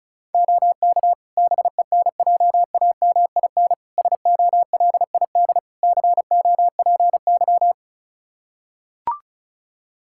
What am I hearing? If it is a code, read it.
Morse code, heard as a keyed tone, about 36 wpm, 700 Hz. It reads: OK BENJAMIN SOLID COPY
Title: K1USN SST QSO Practice 35wpm